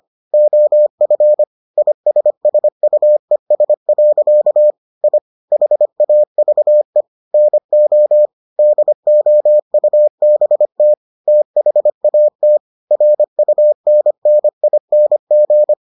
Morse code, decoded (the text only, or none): OF ISSUES. I HAVE NO DOUBT THAT RUNNING